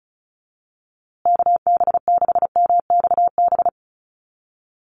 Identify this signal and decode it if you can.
Morse code, heard as a keyed tone, about 34 wpm, 700 Hz. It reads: KB6MXB
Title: USA Call Signs Repeated 2x 35wpm